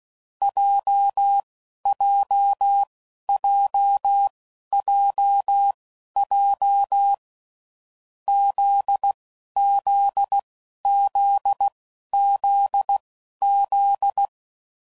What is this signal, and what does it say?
Morse code, heard as a keyed tone, about 16 wpm, spaced as for 11 wpm, 800 Hz. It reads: JJJJJ ZZZZZ
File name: Groups / Basic Letter Groups / JZ17